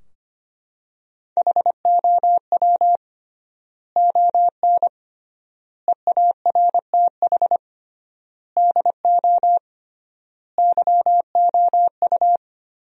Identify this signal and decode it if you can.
Morse code, heard as a keyed tone, about 25 wpm, 700 Hz. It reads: HOW ON EARTH DO YOU
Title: 3x Word-Spacing - Sentences from Top 200 Words - 25wpm